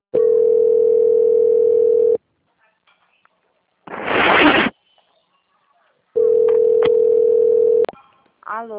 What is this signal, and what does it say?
Ringback tone (North America), 440 + 480 Hz, with speech.
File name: out- - - - - .11103